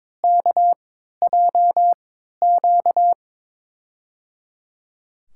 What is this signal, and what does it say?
Morse code, heard as a keyed tone, about 22 wpm, spaced as for 13 wpm, 700 Hz. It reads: KJQ
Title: USA Call Signs - 3x Character Spacing - 22wpm